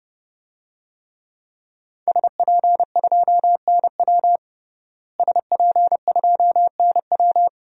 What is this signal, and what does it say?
Morse code, heard as a keyed tone, about 32 wpm, 700 Hz. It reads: SP2NW SP2NW
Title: DXCC-100 Call Signs - 5 Characters - Repeated 2x - 30wpm